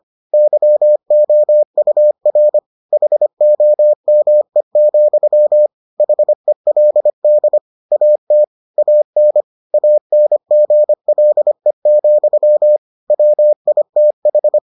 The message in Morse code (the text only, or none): YOUR HOME, HELD AT AN ANGLE, WITH